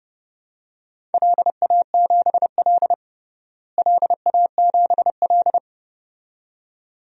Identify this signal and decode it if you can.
Morse code, heard as a keyed tone, about 30 wpm, 700 Hz. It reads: LA7L LA7L
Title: DXCC-100 Call Signs - 4 Characters - Repeated 2x - 30wpm